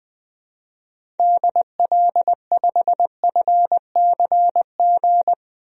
Morse code, heard as a keyed tone, about 20 wpm, 700 Hz. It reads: DL5FCG
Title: DXCC-100 Call Signs - 6 Characters - Repeated 2x - 20wpm